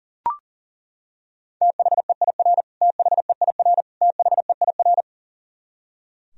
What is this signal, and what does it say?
Morse code, heard as a keyed tone, about 37 wpm, 700 Hz. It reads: THEIR THEIR THEIR
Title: Top 100 Words Repeated 40wpm